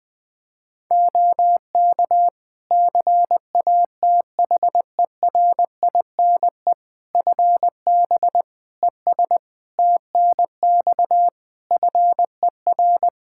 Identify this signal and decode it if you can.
Morse code, heard as a keyed tone, about 20 wpm, 700 Hz. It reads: OK CATHERINE FB ES TNX FER
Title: Practice QSO - Exchange 2 - Info - 20wpm